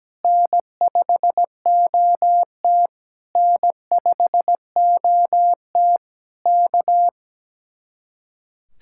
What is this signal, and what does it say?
Morse code, heard as a keyed tone, about 17 wpm, 700 Hz. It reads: N5OT N5OT K
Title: Speed-Racing Common QSO Phrases - 17wpm